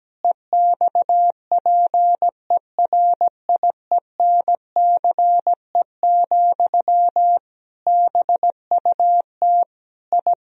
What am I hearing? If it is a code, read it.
Morse code, heard as a keyed tone, about 17 wpm, 700 Hz. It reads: EXPERIENCE, BUT I
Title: Sentences from Top 1000 Words - Review Part 4 - 17wpm